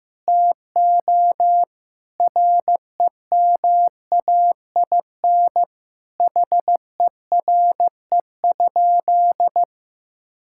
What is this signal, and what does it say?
Morse code, heard as a keyed tone, about 15 wpm, 700 Hz. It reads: TO REMAIN HERE?
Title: Sentences from Top 1600 Words 15wpm